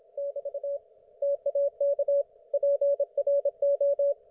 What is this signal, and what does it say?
Morse code, heard as a keyed tone, about 27 wpm, 560 Hz, with radio noise. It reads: <BT> TAK PRO